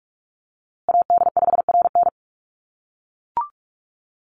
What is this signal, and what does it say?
Morse code, heard as a keyed tone, about 44 wpm, 700 Hz. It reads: AD5RN
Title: USA Call Signs Repeated 2x 45wpm